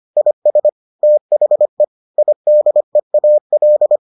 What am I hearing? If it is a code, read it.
Morse code, heard as a keyed tone, about 25 wpm, 600 Hz. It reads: IS THE IDEAL